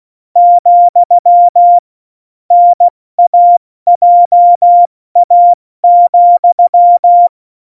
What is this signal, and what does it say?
Morse code, heard as a keyed tone, about 16 wpm, 700 Hz. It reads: , NAJA,
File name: Morsekurs / Witze / Witz 20. August 2019, 70 F80